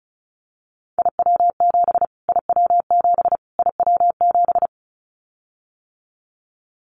Morse code, heard as a keyed tone, about 34 wpm, 700 Hz. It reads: IW7 IW7 IW7
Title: Callsign Prefixes for Top 25 DXCC - Repeated 3x - 35wpm